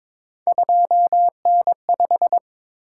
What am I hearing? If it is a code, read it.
Morse code, heard as a keyed tone, about 22 wpm, 700 Hz. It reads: 2N5
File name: practice / 22wpm / Three Letter-Number Combinations - 22wpm